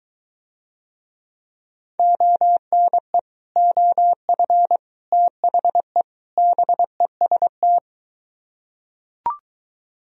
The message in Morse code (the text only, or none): ONE OF THE BEST